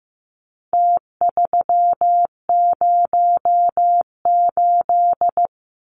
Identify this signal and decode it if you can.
Morse code, heard as a keyed tone, about 14 wpm, 700 Hz. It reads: T308
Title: Numbers - Contest Style - 4-Digit 15wpm